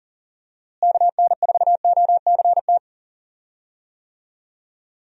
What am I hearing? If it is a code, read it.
Morse code, heard as a keyed tone, about 40 wpm, 700 Hz. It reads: KN4OCT
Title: USA Call Signs - 6 Characters - Repeated 3x - 40wpm